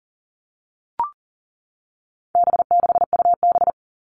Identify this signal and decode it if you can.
Morse code, heard as a keyed tone, about 40 wpm, 700 Hz. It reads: B6UB
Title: All Cut Numbers - 4-Digit 40wpm